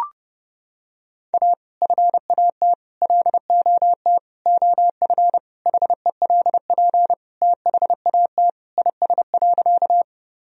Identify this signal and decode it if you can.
Morse code, heard as a keyed tone, about 30 wpm, 700 Hz. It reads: A FAT LOT OF HELP THAT IS.
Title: Sentences from Top 1000 Words - Review Part 4 - 30wpm